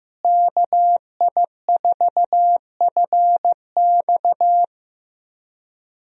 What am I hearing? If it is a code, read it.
Morse code, heard as a keyed tone, about 15 wpm, 700 Hz. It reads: KI4FX
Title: USA Call Signs - 5 Characters - Repeated 3x - 15wpm